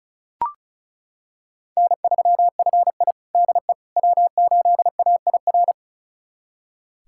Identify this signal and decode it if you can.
Morse code, heard as a keyed tone, about 36 wpm, 700 Hz. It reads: N3FI DE W8AIR
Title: USA Callsign DE Callsign 35wpm